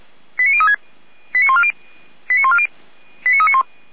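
Five-tone selective call (ZVEI), heard as quick runs of five tones, tones of about 80 ms. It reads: 89036 89140 89140 89391
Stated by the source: ZVEI five-tone call received on an SDR.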